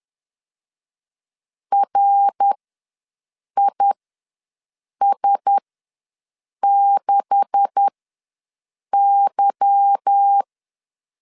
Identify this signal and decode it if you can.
Morse code, heard as a keyed tone, about 10 wpm, spaced as for 6 wpm, 800 Hz. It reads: RIS6Y
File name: LEZIONI / 9 286 ws 10wpm 4032